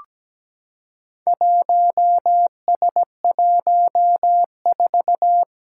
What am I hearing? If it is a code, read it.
Morse code, heard as a keyed tone, about 17 wpm, 700 Hz. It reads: 1S14
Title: Four Challenging Letter-Number Combinations 17wpm